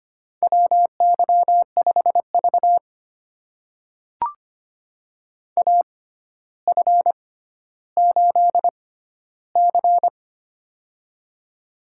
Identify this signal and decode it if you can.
Morse code, heard as a keyed tone, about 25 wpm, spaced as for 13 wpm, 700 Hz. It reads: WY5V AF8C
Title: USA Call Signs - 4 Characters - 6x Character Spacing - 25wpm